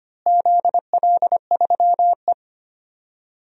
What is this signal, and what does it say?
Morse code, heard as a keyed tone, about 25 wpm, 700 Hz. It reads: ZL3E
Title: DXCC-100 Call Signs - 4 Characters - Repeated 3x - 25wpm